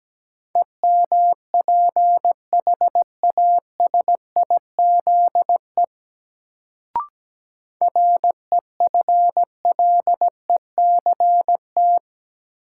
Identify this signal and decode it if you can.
Morse code, heard as a keyed tone, about 17 wpm, 700 Hz. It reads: EMPHASIZE REFLECT
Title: Sentences from Top 1500 Words - New Word Drill Alt - 17wpm